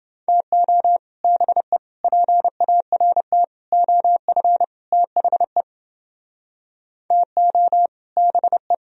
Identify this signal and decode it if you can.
Morse code, heard as a keyed tone, about 30 wpm, 700 Hz. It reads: TO BE PART OF THE TO BE